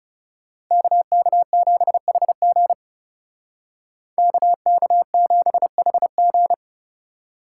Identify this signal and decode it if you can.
Morse code, heard as a keyed tone, about 37 wpm, 700 Hz. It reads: KK7HG KK7HG
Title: Speed Racing - USA 2x2 Call Signs 30wpm